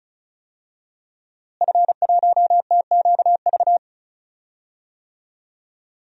Morse code, heard as a keyed tone, about 36 wpm, 700 Hz. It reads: F1TQV
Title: DXCC-100 Call Signs - 5 Characters - Repeated 2x - 35wpm